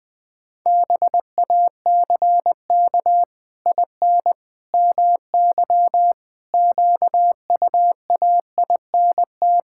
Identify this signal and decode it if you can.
Morse code, heard as a keyed tone, about 20 wpm, 700 Hz. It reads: BACK IN MY QUAINT
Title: Pangram Challenge v2 - 20wpm